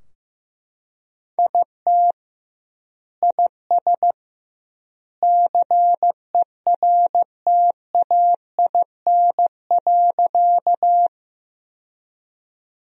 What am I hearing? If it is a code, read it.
Morse code, heard as a keyed tone, about 15 wpm, 700 Hz. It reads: IT IS CERTAIN.
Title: 2x Word-Spacing - Sentences from Top 500 Words - 15wpm